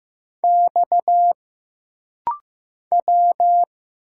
Morse code, heard as a keyed tone, about 15 wpm, 700 Hz. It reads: X W
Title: Single Letters - Rapid-Fire 15wpm